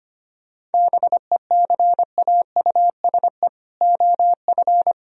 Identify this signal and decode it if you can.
Morse code, heard as a keyed tone, about 30 wpm, 700 Hz. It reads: BECAUSE OF
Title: Speed-Racing - Sets of 2 Words 25wpm